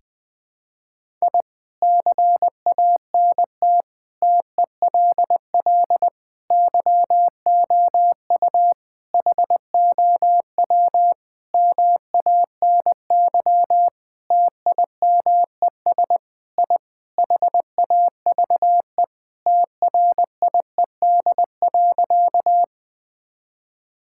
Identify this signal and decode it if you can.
Morse code, heard as a keyed tone, about 20 wpm, 700 Hz. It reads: I CANT TELL YOU HOW MANY TIMES I HAVE TRIED.
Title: Sentences with 6-Word Phrases - Spoken Beginning - 20wpm